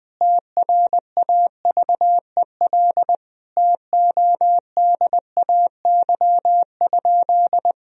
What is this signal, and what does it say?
Morse code, heard as a keyed tone, about 20 wpm, 700 Hz. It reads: TRAVEL TODAY?
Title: Sentences from Top 1000 Words - Review Part 2 - 20wpm